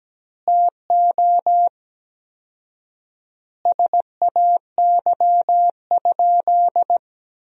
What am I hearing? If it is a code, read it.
Morse code, heard as a keyed tone, about 17 wpm, 700 Hz. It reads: TO SAY?
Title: 4x Word-Spacing - Sentences from Top 200 Words - 17wpm